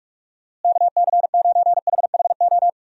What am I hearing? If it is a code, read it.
Morse code, heard as a keyed tone, about 46 wpm, 700 Hz. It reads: KC9HHO
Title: USA Call Signs - 6 Characters - Repeated 2x - 45wpm